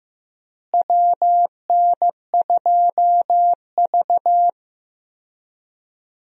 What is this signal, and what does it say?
Morse code, heard as a keyed tone, about 15 wpm, 700 Hz. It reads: WN2V
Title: USA Call Signs - 4 Characters - Repeated 3x - 15wpm